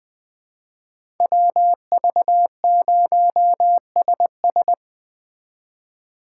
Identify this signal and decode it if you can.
Morse code, heard as a keyed tone, about 21 wpm, 700 Hz. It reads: WV0SS
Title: Speed Racing - USA 2x2 Call Signs 17wpm